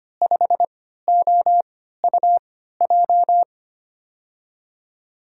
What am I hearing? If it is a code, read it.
Morse code, heard as a keyed tone, about 25 wpm, spaced as for 14 wpm, 700 Hz. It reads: 5OUJ
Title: USA Call Signs - 6 Characters - 3x Character Spacing - 25wpm